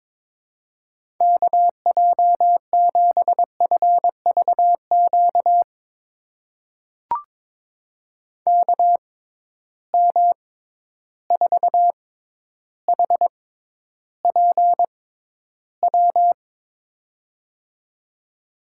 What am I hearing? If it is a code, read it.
Morse code, heard as a keyed tone, about 22 wpm, spaced as for 11 wpm, 700 Hz. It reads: KJ7FVQ KM4HPW
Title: USA Call Signs - 6 Characters - 6x Character Spacing - 22wpm